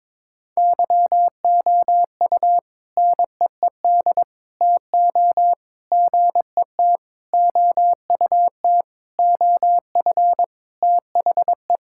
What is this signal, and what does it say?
Morse code, heard as a keyed tone, about 22 wpm, 700 Hz. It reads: YOU NEED TO GET OUT OF THE
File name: practice / 22wpm / Sentences with 4-Word Phrases - 22wpm